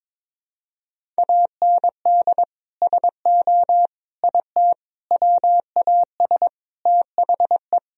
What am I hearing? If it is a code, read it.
Morse code, heard as a keyed tone, about 21 wpm, 700 Hz. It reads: AND SO IT WAS THE
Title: Sentences from Top 1000 Words - Review Part 1 - 22wpm